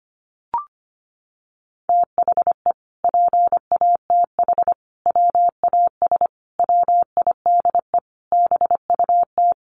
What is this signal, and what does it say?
Morse code, heard as a keyed tone, about 25 wpm, 700 Hz. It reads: THE PATH WAS WIDE BUT
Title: Sentences from Top 1400 Words 25wpm